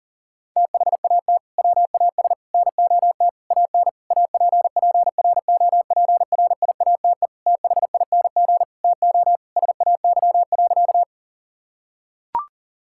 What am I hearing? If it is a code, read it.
Morse code, heard as a keyed tone, about 40 wpm, 700 Hz. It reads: THAT WAS NOT AN APPROPRIATE THING TO SAY.
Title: Sentences from Top 1000 Words - Review Part 5 - 40wpm